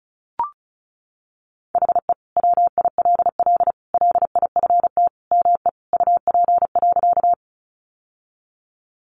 Morse code, heard as a keyed tone, about 34 wpm, 700 Hz. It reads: HE WILL LIFT ME UP.
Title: Sentences from Top 1000 Words 35wpm